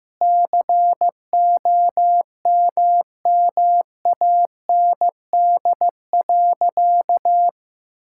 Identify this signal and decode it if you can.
Morse code, heard as a keyed tone, about 15 wpm, 700 Hz. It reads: COMMAND.
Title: Sentences from Top 1600 Words 15wpm